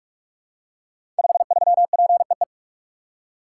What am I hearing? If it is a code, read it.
Morse code, heard as a keyed tone, about 44 wpm, 700 Hz. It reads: 53PEE
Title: Code Groups - 5 Letter:Numbers - 45wpm